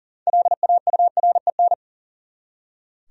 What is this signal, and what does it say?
Morse code, heard as a keyed tone, about 40 wpm, 700 Hz. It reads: LAUREN
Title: USA Common Names - 40wpm